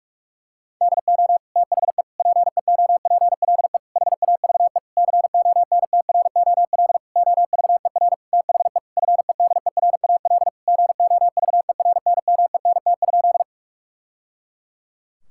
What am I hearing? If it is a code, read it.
Morse code, heard as a keyed tone, about 41 wpm, 700 Hz. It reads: DO THE PEOPLE HAVE CONTROL OVER THE FEDERAL GOVERNMENT?
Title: Sentences from Top 1000 Words - Review Part 5 - 45wpm